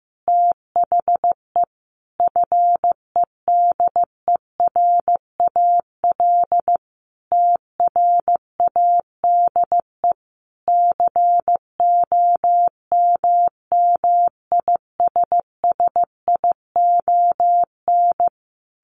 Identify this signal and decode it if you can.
Morse code, heard as a keyed tone, about 15 wpm, 700 Hz. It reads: THE FEDERAL TRADE COMMISSION
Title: Sentences from Top 1400 Words 15wpm